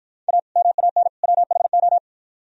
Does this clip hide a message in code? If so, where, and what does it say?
Yes: Morse code from 0.3 to 2.0 s, heard as a keyed tone, about 53 wpm, 700 Hz. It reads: A MAN WHO